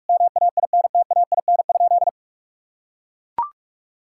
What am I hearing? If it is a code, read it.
Morse code, heard as a keyed tone, about 43 wpm, 700 Hz. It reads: MAINTAIN?
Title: Sentences from Top 1000 Words - Review Part 5 - 45wpm